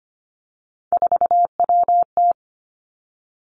Morse code, heard as a keyed tone, about 25 wpm, 700 Hz. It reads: 4WT
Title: Three Challenging Letter-Number Combinations 25wpm